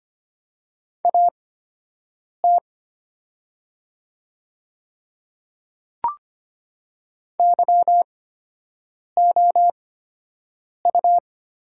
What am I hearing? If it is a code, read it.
Morse code, heard as a keyed tone, about 25 wpm, spaced as for 7 wpm, 700 Hz. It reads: AT YOU
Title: 8x Character Spacing - Top 100 Words - 25wpm